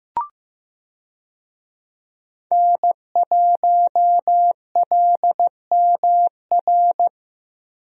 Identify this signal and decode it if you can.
Morse code, heard as a keyed tone, about 15 wpm, 700 Hz. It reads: N1LMR
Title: USA Call Signs - 5 Characters - Repeated 2x - 15wpm